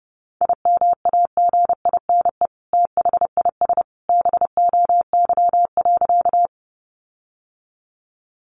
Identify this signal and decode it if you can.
Morse code, heard as a keyed tone, about 30 wpm, 700 Hz. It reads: IMAGINE THIS BOY.
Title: Sentences from Top 1400 Words 30wpm